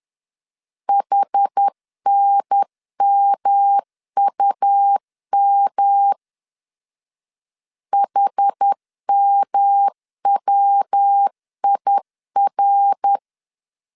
Morse code, heard as a keyed tone, about 10 wpm, 800 Hz. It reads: HNMUM HMWIR
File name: 10wpm